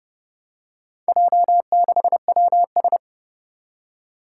Morse code, heard as a keyed tone, about 30 wpm, 700 Hz. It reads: J6WS
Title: Four Challenging Letter-Number Combinations 30wpm